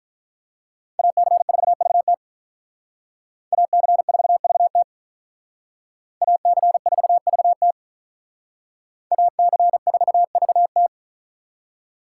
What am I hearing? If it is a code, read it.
Morse code, heard as a keyed tone, about 45 wpm, 700 Hz. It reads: AC4VT AC4VT AC4VT AC4VT